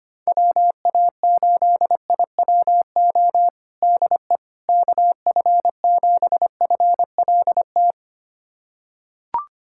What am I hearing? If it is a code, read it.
Morse code, heard as a keyed tone, about 25 wpm, 700 Hz. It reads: WA8IWO DE KF7FLT
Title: USA Callsign DE Callsign 25wpm